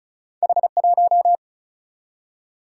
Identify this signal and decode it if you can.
Morse code, heard as a keyed tone, about 35 wpm, 700 Hz. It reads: H1
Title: Single-Letter Single-Number 35wpm